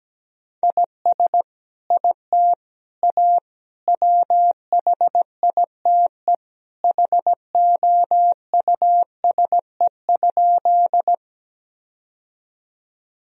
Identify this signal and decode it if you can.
Morse code, heard as a keyed tone, about 17 wpm, 700 Hz. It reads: IS IT A WHITE HOUSE?
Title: Sentences from Top 1000 Words - Review Part 1 - 17wpm